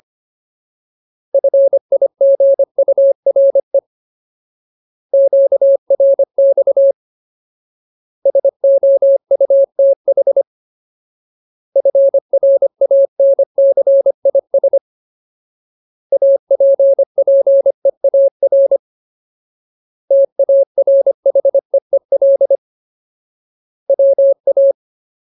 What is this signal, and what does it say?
Morse code, heard as a keyed tone, about 25 wpm, 550 Hz. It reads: FIGURE QRX SOUTH FRANCIS APPEAR TARHEEL WA
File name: Ultimate Word List / 25WPM Ultimate